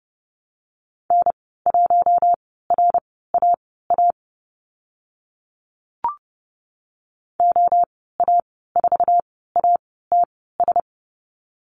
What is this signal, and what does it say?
Morse code, heard as a keyed tone, about 30 wpm, spaced as for 17 wpm, 700 Hz. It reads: N1RAA OA4ATS
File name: Lesson 15g - Number 5 - Call Signs 30wpm